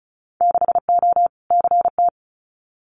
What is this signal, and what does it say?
Morse code, heard as a keyed tone, about 34 wpm, 700 Hz. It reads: 6O CT
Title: Winter Field Day Exchanges Encore 35wpm